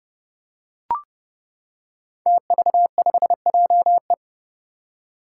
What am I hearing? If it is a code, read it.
Morse code, heard as a keyed tone, about 30 wpm, 700 Hz. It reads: TV5JE
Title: Five Challenging Letter-Number Combinations 30wpm